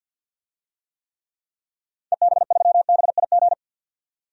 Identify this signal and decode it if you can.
Morse code, heard as a keyed tone, about 52 wpm, 700 Hz. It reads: EB3BIG
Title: DXCC-100 Call Signs - 6 Characters - 50wpm